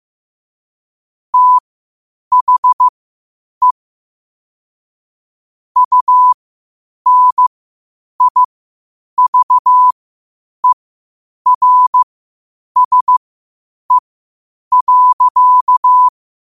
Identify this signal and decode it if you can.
Morse code, heard as a keyed tone, about 15 wpm, spaced as for 8 wpm, 1000 Hz. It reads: THE UNIVERSE.
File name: qotdfiles / 8wpm